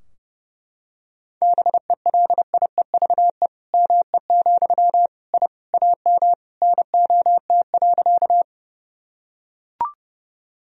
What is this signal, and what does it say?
Morse code, heard as a keyed tone, about 30 wpm, 700 Hz. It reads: BELIEVE ME, I AM NOT.
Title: Sentences from Top 1000 Words - Review Part 2 - 30wpm